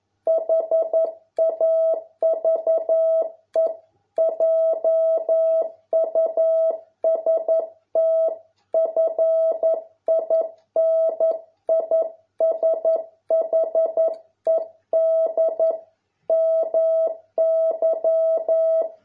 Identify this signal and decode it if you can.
Morse code, heard as a keyed tone, about 11 wpm, 650 Hz. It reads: HAVE JUST FINISHED MY